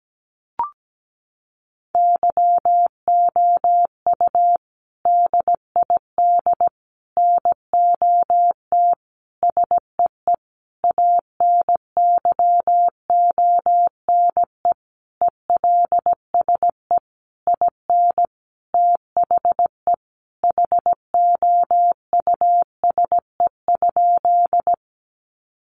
Morse code, heard as a keyed tone, about 17 wpm, 700 Hz. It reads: YOU DID NOT SEE ANYONE ELSE IN THE HOUSE?
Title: Sentences from Top 900 Words 17wpm